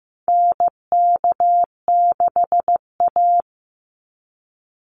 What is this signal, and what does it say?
Morse code, heard as a keyed tone, about 15 wpm, 700 Hz. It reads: NK6A
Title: NA SOTA Chaser Callsigns 15wpm